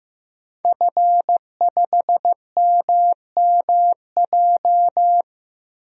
Morse code, heard as a keyed tone, about 15 wpm, 700 Hz. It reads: F5MMJ
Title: DXCC-100 Call Signs - 5 Characters - 15wpm